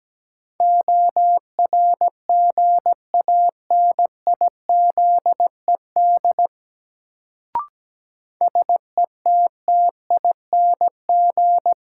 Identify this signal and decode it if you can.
Morse code, heard as a keyed tone, about 17 wpm, 700 Hz. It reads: ORGANIZED SETTING
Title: Sentences from Top 1300 Words - New Word Drill Alt - 17wpm